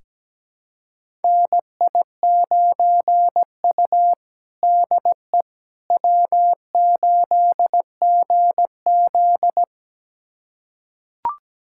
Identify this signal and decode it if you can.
Morse code, heard as a keyed tone, about 17 wpm, 700 Hz. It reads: NI9U DE W8GZ
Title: USA Callsign DE Callsign - 4 Characters - 17wpm